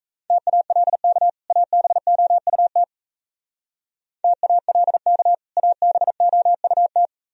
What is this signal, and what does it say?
Morse code, heard as a keyed tone, about 32 wpm, 700 Hz. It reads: TALK ABOUT TALK ABOUT
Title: Speed-Racing - Sets of 2 Words 28wpm